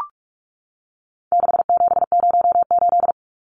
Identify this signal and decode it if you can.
Morse code, heard as a keyed tone, about 44 wpm, 700 Hz. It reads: 6708
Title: Numbers - 4-Digit 45wpm